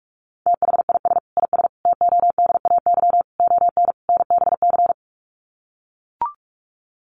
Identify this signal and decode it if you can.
Morse code, heard as a keyed tone, about 44 wpm, 700 Hz. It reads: THIS IS TODAY ON NBC
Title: Sets of 5 Words Encore 45wpm